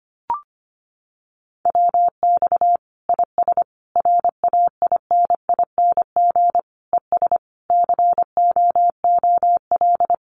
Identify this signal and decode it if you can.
Morse code, heard as a keyed tone, about 25 wpm, 700 Hz. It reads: WX IS RAINING ES COOL
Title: Common QSO Phrases 25wpm